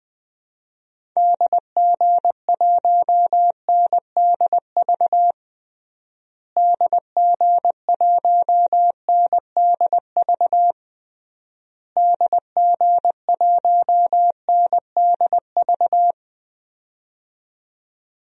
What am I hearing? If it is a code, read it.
Morse code, heard as a keyed tone, about 20 wpm, 700 Hz. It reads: DG1NDV DG1NDV DG1NDV
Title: DXCC-100 Call Signs - 6 Characters - Repeated 3x - 20wpm